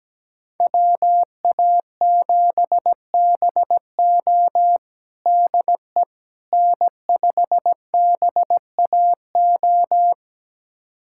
Morse code, heard as a keyed tone, about 17 wpm, 700 Hz. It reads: WA7BO DE N5BAO
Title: USA Callsign DE Callsign - 5 Characters - 17wpm